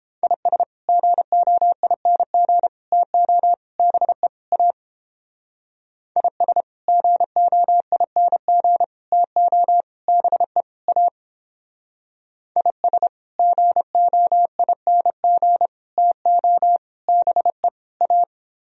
Morse code, heard as a keyed tone, about 29 wpm, 700 Hz. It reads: IS GOING TO BE A IS GOING TO BE A IS GOING TO BE A